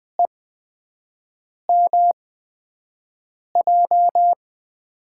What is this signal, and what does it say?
Morse code, heard as a keyed tone, about 20 wpm, spaced as for 5 wpm, 700 Hz. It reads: EMJ
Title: USA Call Signs - 6 Characters - 8x Character Spacing - 20wpm